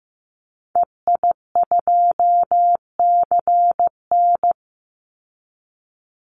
Morse code, heard as a keyed tone, about 15 wpm, 700 Hz. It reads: EI2CN
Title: Call Signs Repeated 3x - 15wpm